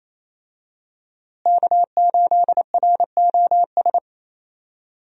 Morse code, heard as a keyed tone, about 28 wpm, 700 Hz. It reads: K8ROS
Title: USA Call Signs - 5 Characters - Repeated 3x - 28wpm